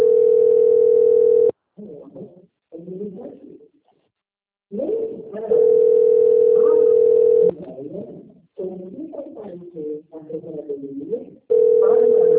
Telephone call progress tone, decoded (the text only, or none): ringback tone